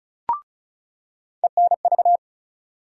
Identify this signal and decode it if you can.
Morse code, heard as a keyed tone, about 36 wpm, 700 Hz. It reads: ENV
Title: USA Call Sign Postfixes - 3 Letters 35wpm